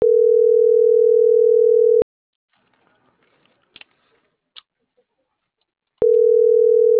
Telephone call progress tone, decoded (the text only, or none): ringback tone